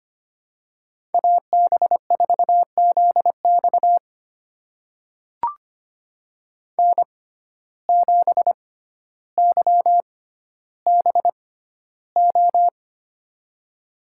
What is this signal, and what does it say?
Morse code, heard as a keyed tone, about 25 wpm, spaced as for 13 wpm, 700 Hz. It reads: AB4ZX N7YBO
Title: USA Call Signs - 5 Characters - 6x Character Spacing - 25wpm